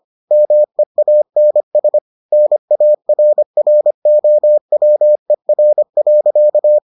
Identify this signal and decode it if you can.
Morse code, heard as a keyed tone, about 25 wpm, 600 Hz. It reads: MEANS NARROWER.